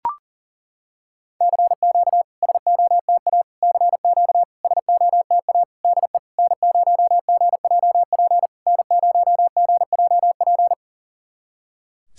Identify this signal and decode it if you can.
Morse code, heard as a keyed tone, about 40 wpm, 700 Hz. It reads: CQ SOTA CQ SOTA DE N0GJP N0GJP
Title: SOTA QSO Practice 40wpm